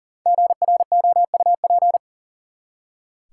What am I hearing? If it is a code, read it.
Morse code, heard as a keyed tone, about 40 wpm, 700 Hz. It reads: GROUP
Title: Five Letter Words - 40wpm